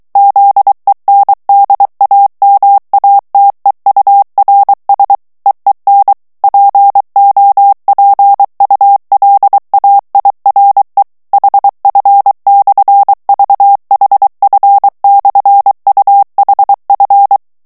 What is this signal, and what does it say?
Morse code, heard as a keyed tone, about 23 wpm, 800 Hz. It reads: ZENDAMATEURS EEN POPULAIRE HF/VHF/UHF
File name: HOLLANDAIS texte clair 1 vitesse 22 mots minute